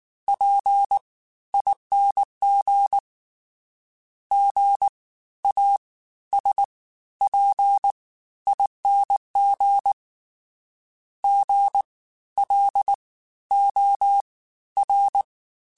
Morse code, heard as a keyed tone, about 19 wpm, spaced as for 13 wpm, 800 Hz. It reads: PING GASPING GLOR
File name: Special INGs-mp3s / Special INGs-mp3s / Fast ing @18